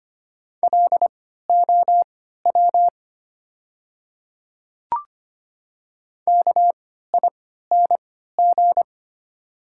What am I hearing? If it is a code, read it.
Morse code, heard as a keyed tone, about 25 wpm, spaced as for 14 wpm, 700 Hz. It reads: LOW KING
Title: 3x Character Spacing - Top 500 Words - 25wpm